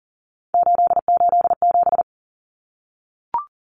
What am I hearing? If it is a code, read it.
Morse code, heard as a keyed tone, about 40 wpm, 700 Hz. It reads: 887
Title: Common Cut Numbers - 3-Digit 40wpm